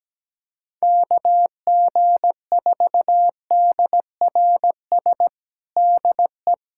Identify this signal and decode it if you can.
Morse code, heard as a keyed tone, about 17 wpm, 700 Hz. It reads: KG4DRS DE
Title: USA Callsign DE Callsign - 6 Characters - 17wpm